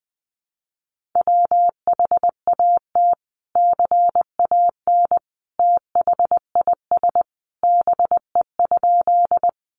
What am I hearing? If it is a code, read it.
Morse code, heard as a keyed tone, about 20 wpm, 700 Hz. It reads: WHAT CAN THIS BE?
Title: Sentences from Top 100 Words 20wpm